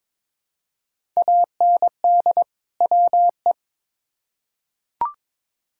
Morse code, heard as a keyed tone, about 22 wpm, 700 Hz. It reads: AND WE
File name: Build A Phrase - 2-Word Phrases - 22wpm